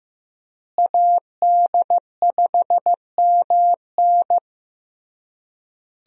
Morse code, heard as a keyed tone, about 15 wpm, 700 Hz. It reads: AD5MN
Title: USA Call Signs - 5 Characters - Repeated 3x - 15wpm